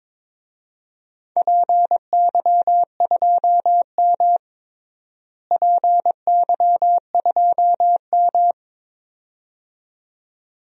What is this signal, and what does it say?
Morse code, heard as a keyed tone, about 22 wpm, 700 Hz. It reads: PY2M PY2M
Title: DXCC-100 Call Signs - 4 Characters - Repeated 2x - 22wpm